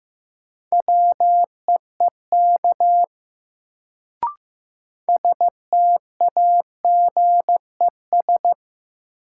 Morse code, heard as a keyed tone, about 15 wpm, 700 Hz. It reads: WEEK STAGES
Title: Sets of 5 Words Encore - New Word Drill - 15wpm